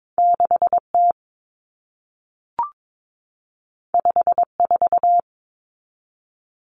Morse code, heard as a keyed tone, about 22 wpm, 700 Hz. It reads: 6T 54
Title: Common Cut Numbers - 2-Digit 22wpm